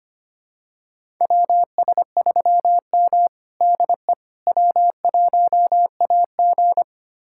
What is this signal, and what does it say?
Morse code, heard as a keyed tone, about 25 wpm, 700 Hz. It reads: WS3M DE W1AG
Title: USA Callsign DE Callsign - 4 Characters - 25wpm